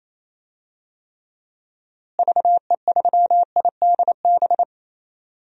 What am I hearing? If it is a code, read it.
Morse code, heard as a keyed tone, about 28 wpm, 700 Hz. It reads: VE3IDB
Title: DXCC-100 Call Signs - 6 Characters - Repeated 2x - 28wpm